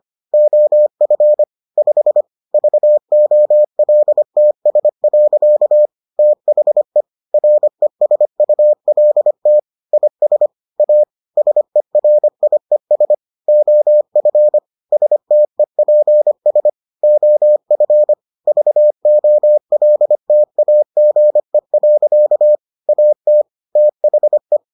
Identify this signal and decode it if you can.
Morse code, heard as a keyed tone, about 25 wpm, 600 Hz. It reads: OF 5 VOLTS. THE RESULT IS A SERIES OF STEPS OF VOLTAGE. AT THE